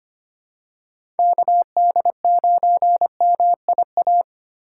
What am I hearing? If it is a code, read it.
Morse code, heard as a keyed tone, about 25 wpm, 700 Hz. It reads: KD9MIA
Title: USA Call Signs - 6 Characters - 2x Character Spacing - 25wpm